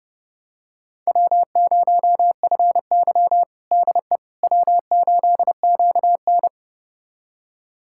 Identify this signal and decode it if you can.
Morse code, heard as a keyed tone, about 30 wpm, 700 Hz. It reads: W0FY DE W8QN
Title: USA Callsign DE Callsign - 4 Characters - 30wpm